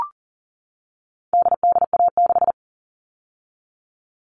Morse code, heard as a keyed tone, about 40 wpm, 700 Hz. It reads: DDA6
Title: All Cut Numbers - 4-Digit 40wpm